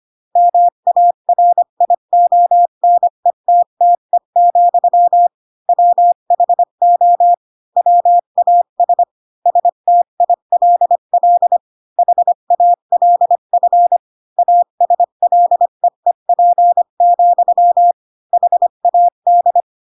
Morse code, heard as a keyed tone, about 25 wpm, 700 Hz. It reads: MARIONETTE, WHO WAS STILL HALF ASLEEP, HAD
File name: Pinocchio / Pinocchio ch7 20 25 ik1hge